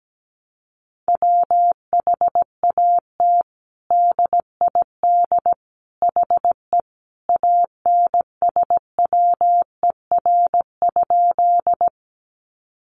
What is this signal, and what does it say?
Morse code, heard as a keyed tone, about 17 wpm, 700 Hz. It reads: WHAT DID HE ANSWER?
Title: Sentences from Top 300 Words 17wpm